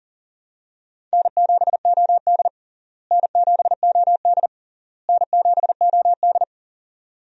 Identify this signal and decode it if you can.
Morse code, heard as a keyed tone, about 40 wpm, 700 Hz. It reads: N7OD N7OD N7OD
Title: USA Call Signs - 4 Characters - Repeated 3x - 40wpm